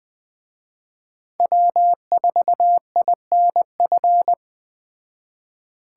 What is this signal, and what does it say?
Morse code, heard as a keyed tone, about 20 wpm, 700 Hz. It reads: W4INF
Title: Speed Racing - USA 1x3 Call Signs 17wpm